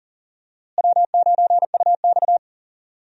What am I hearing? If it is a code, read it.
Morse code, heard as a keyed tone, about 37 wpm, 700 Hz. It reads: W9UX
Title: USA Call Signs - 4 Characters - Repeated 3x - 40wpm